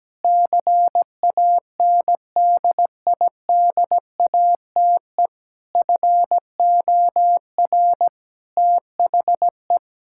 Morse code, heard as a keyed tone, about 17 wpm, 700 Hz. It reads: CANDIDATE FOR THE
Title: Sentences from Top 1600 Words 17wpm